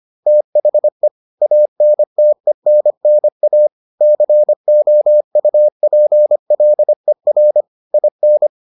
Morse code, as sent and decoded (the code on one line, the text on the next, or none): - .... / .- -. - . -. -. .- / -.-. --- ..- .--. .-.. .-. / .. -.
THE ANTENNA COUPLER IN